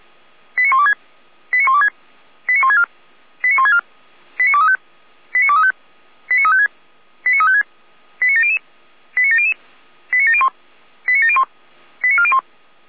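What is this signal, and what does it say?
Five-tone selective call (ZVEI), heard as quick runs of five tones, tones of about 80 ms. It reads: 89126 89126 89164 89164 89235 89235 89356 89356 89700 89700 89701 89701 89401
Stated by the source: ZVEI five-tone selective call 89338 received by SDR.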